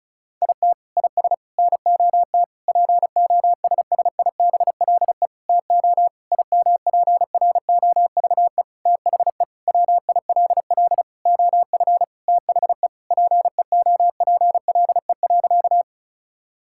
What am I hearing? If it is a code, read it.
Morse code, heard as a keyed tone, about 36 wpm, 700 Hz. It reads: IT IS NOT POSSIBLE TO IMPROVE THE WILL OF THE PEOPLE.
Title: Sentences from Top 1000 Words - Review Part 5 - 35wpm